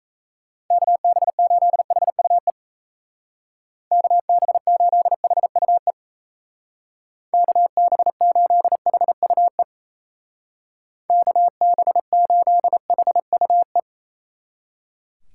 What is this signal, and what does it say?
Morse code, heard as a keyed tone, about 37 wpm, 700 Hz. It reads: KB8HUE KB8HUE KB8HUE KB8HUE